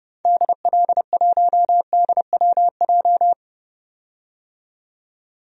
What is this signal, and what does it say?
Morse code, heard as a keyed tone, about 30 wpm, 700 Hz. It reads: DL1DWJ
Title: DXCC-100 Call Signs - 6 Characters - 30wpm